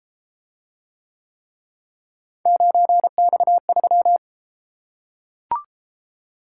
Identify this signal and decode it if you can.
Morse code, heard as a keyed tone, about 33 wpm, 700 Hz. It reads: 9X3